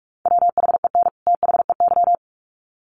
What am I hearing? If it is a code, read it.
Morse code, heard as a keyed tone, about 44 wpm, 700 Hz. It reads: WHEN THEY
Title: Sets of 2 Words 45wpm